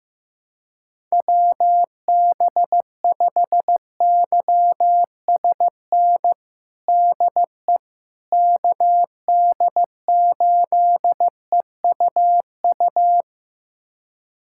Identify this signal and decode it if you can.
Morse code, heard as a keyed tone, about 15 wpm, 700 Hz. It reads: WB5YSN DE KD8EUU
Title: USA Callsign DE Callsign - 6 Characters - 15wpm